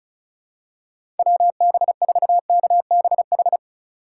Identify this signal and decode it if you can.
Morse code, heard as a keyed tone, about 36 wpm, 700 Hz. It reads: WB4KBH
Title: USA Call Signs - 6 Characters - 2x Character Spacing - 35wpm